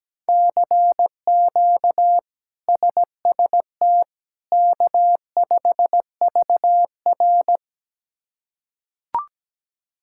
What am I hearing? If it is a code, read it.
Morse code, heard as a keyed tone, about 17 wpm, 700 Hz. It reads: CQ SST K5VR
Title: K1USN SST QSO Practice - 17wpm